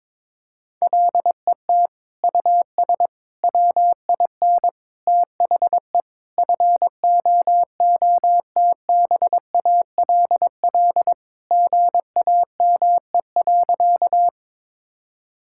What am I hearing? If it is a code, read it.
Morse code, heard as a keyed tone, about 22 wpm, 700 Hz. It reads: LET US WIN THE FOOTBALL GAME.
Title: Sentences from Top 1600 Words 22wpm